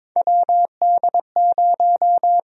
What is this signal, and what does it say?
Morse code, heard as a keyed tone, about 22 wpm, 700 Hz. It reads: WD0
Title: USA Call Sign Prefixes - Repeated 2x - 22wpm